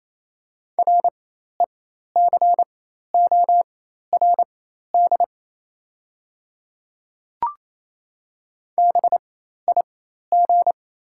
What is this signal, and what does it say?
Morse code, heard as a keyed tone, about 27 wpm, spaced as for 13 wpm, 690 Hz. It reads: RECORD BIG
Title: 4x Character Spacing - Top 500 Words - 28wpm